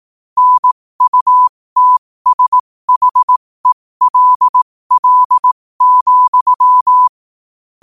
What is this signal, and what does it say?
Morse code, heard as a keyed tone, about 18 wpm, spaced as for 15 wpm, 1000 Hz. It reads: NUTSHELL,
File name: qotdfiles / 15wpm